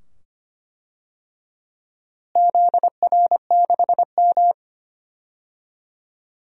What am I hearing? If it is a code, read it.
Morse code, heard as a keyed tone, about 25 wpm, 700 Hz. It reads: ZR6M
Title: DXCC-100 Call Signs - 4 Characters - Repeated 2x - 25wpm